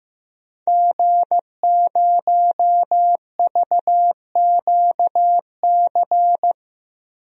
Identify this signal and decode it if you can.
Morse code, heard as a keyed tone, about 15 wpm, 700 Hz. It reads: G0VQC
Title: DXCC-100 Call Signs - 5 Characters - Repeated 2x - 15wpm